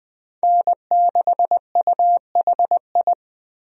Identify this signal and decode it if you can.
Morse code, heard as a keyed tone, about 21 wpm, 700 Hz. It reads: N6UHI
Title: USA Call Signs - 5 Characters - 20wpm